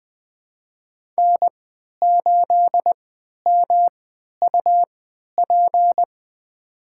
Morse code, heard as a keyed tone, about 20 wpm, spaced as for 11 wpm, 700 Hz. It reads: N8MUP
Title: USA Call Signs - 5 Characters - 3x Character Spacing - 20wpm